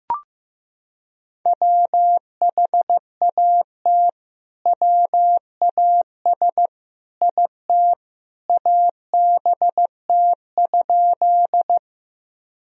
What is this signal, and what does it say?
Morse code, heard as a keyed tone, about 15 wpm, 700 Hz. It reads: WHAT WAS IT ABT?
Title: Sentences from Top 100 Words and Abbreviations 15wpm